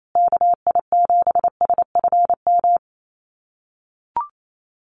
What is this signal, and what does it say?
Morse code, heard as a keyed tone, about 28 wpm, 700 Hz. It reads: KI7SFM
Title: USA Call Signs Repeated 3x 28wpm